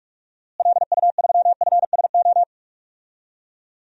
Morse code, heard as a keyed tone, about 43 wpm, 700 Hz. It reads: LU3FSO
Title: DXCC-100 Call Signs - 6 Characters - Repeated 2x - 45wpm